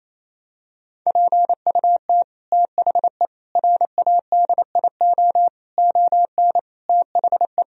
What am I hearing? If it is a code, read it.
Morse code, heard as a keyed tone, about 27 wpm, 700 Hz. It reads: PUT THE RADIO ON THE
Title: Sentences from Top 1000 Words - Review Part 3 - 28wpm